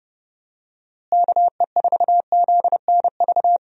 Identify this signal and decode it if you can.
Morse code, heard as a keyed tone, about 32 wpm, 700 Hz. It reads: KE4ZNV
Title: USA Call Signs - 6 Characters - Repeated 3x - 30wpm